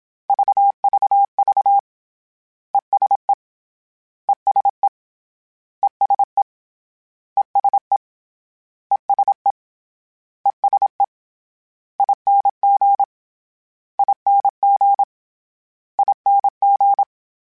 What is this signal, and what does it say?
Morse code, heard as a keyed tone, about 26 wpm, 800 Hz. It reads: VVV ESE ESE ESE ESE ESE ESE ING ING ING
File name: Short Runs / Short Runs-mp3s / ESE ING @25